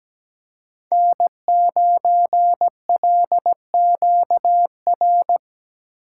Morse code, heard as a keyed tone, about 17 wpm, 700 Hz. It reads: N9LQR
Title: USA Call Signs - 3x Character Spacing - 17wpm